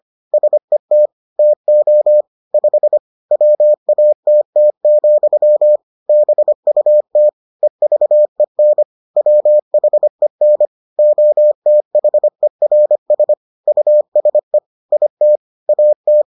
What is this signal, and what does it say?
Morse code, heard as a keyed tone, about 25 wpm, 600 Hz. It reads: SET TO 5 WATT, BUT EVEN WHEN OTHERS USE IT AT